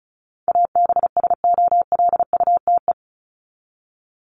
Morse code, heard as a keyed tone, about 34 wpm, 700 Hz. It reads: ABSOLUTE
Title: LY Suffix - Top 100 Words with Base Words 35wpm